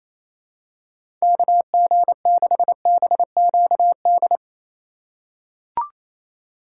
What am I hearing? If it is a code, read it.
Morse code, heard as a keyed tone, about 28 wpm, 700 Hz. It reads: KG6BQD
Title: USA Call Signs - 6 Characters - 6x Character Spacing - 28wpm